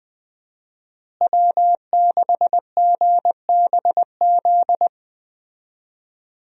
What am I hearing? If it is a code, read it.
Morse code, heard as a keyed tone, about 20 wpm, 700 Hz. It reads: W6GBZ
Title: USA Call Signs - 5 Characters - Repeated 2x - 20wpm